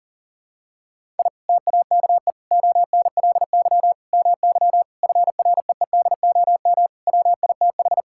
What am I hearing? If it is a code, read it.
Morse code, heard as a keyed tone, about 40 wpm, 700 Hz. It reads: I TAKE ONLY MY FREEDOM WITH
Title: Sentences from Top 1000 Words - Review Part 5 - 40wpm